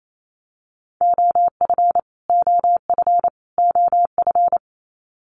Morse code, heard as a keyed tone, about 28 wpm, 700 Hz. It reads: OF OF OF
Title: Top 500 Words Repeated 28wpm